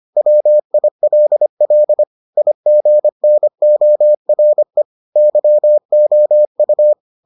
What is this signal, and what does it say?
Morse code, heard as a keyed tone, about 25 wpm, 600 Hz. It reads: WILL IGNORE YOU